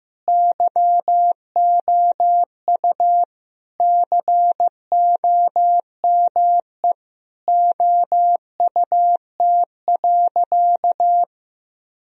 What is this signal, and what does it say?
Morse code, heard as a keyed tone, about 15 wpm, 700 Hz. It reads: YOU COME OUT.
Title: Sentences from Top 1000 Words - Review Part 1 - 15wpm